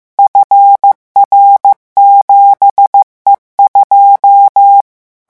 Morse code, heard as a keyed tone, about 15 wpm, 800 Hz. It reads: FR7E2
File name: morse mp3 / revision13-vitesse 14 mots